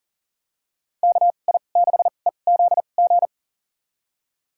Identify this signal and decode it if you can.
Morse code, heard as a keyed tone, about 40 wpm, spaced as for 29 wpm, 700 Hz. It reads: KI6EZG
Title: USA Call Signs - 2x Character Spacing - 40wpm